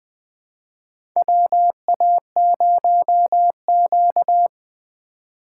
Morse code, heard as a keyed tone, about 20 wpm, 700 Hz. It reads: WA0Q
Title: USA Call Signs - 3x Character Spacing - 20wpm